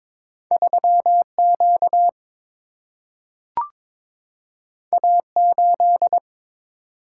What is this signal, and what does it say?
Morse code, heard as a keyed tone, about 22 wpm, 700 Hz. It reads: 3Q A8
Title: Single-Letter Single-Number 22wpm